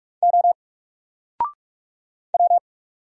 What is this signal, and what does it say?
Morse code, heard as a keyed tone, about 44 wpm, 700 Hz. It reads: O W
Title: Single Letter-Number Alt 45wpm